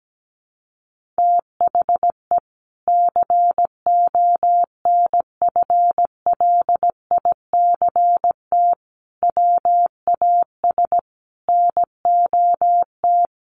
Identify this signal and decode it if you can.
Morse code, heard as a keyed tone, about 17 wpm, 700 Hz. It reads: THE CONFLICT WAS NOT
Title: Sentences from Top 1300 Words 17wpm